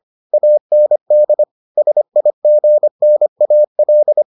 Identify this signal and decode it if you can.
Morse code, heard as a keyed tone, about 25 wpm, 600 Hz. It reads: AND SIGNAL